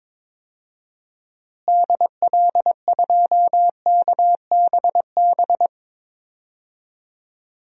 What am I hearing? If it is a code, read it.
Morse code, heard as a keyed tone, about 22 wpm, 700 Hz. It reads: DL2KBB
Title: DXCC-100 Call Signs - 6 Characters - Repeated 3x - 22wpm